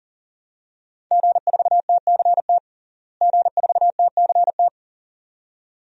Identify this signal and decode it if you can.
Morse code, heard as a keyed tone, about 40 wpm, 700 Hz. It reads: G4TCT G4TCT
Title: DXCC-100 Call Signs - 5 Characters - Repeated 2x - 40wpm